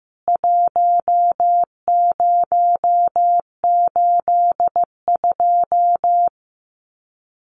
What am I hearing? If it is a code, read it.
Morse code, heard as a keyed tone, about 15 wpm, 700 Hz. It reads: 1082
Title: CWT Exchange 15wpm